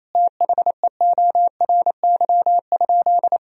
Morse code, heard as a keyed tone, about 28 wpm, 700 Hz. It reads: THEORY?
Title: Sentences from Top 1000 Words - Review Part 3 - 28wpm